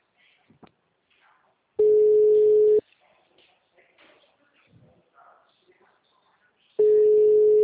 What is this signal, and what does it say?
Ringback tone (Europe, 425 Hz), with speech.